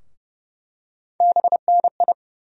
Morse code, heard as a keyed tone, about 30 wpm, 700 Hz. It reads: BNI
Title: USA Call Sign Postfixes - 3 Letters 30wpm